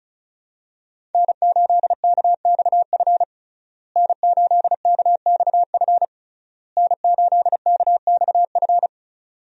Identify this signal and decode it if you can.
Morse code, heard as a keyed tone, about 34 wpm, 700 Hz. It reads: N8KXF N8KXF N8KXF
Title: USA Call Signs - 5 Characters - Repeated 3x - 35wpm